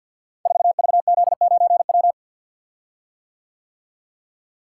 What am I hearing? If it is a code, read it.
Morse code, heard as a keyed tone, about 52 wpm, 700 Hz. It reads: 4VZ9W
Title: Code Groups - 5 Letter:Numbers - 50wpm